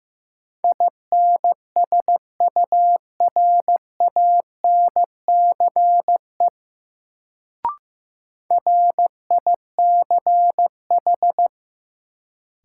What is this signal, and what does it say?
Morse code, heard as a keyed tone, about 16 wpm, 700 Hz. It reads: INSURANCE RICH
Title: Sentences from Top 1100 Words - New Word Drill - 15wpm